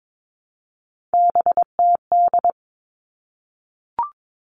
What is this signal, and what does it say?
Morse code, heard as a keyed tone, about 22 wpm, 700 Hz. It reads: BTD
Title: All Cut Numbers - 3-Digit 22wpm